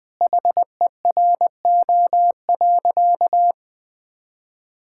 Morse code, heard as a keyed tone, about 20 wpm, 700 Hz. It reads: HERO.
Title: Sentences with 4-Word Phrases - 20wpm